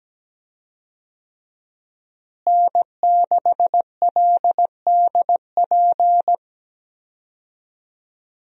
Morse code, heard as a keyed tone, about 17 wpm, 700 Hz. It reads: N6LDP
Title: USA Call Signs - 5 Characters - Repeated 2x - 17wpm